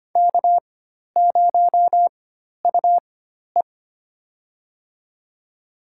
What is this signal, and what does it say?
Morse code, heard as a keyed tone, about 25 wpm, spaced as for 12 wpm, 700 Hz. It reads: K0UE
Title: USA Call Signs - 4 Characters - 4x Character Spacing - 25wpm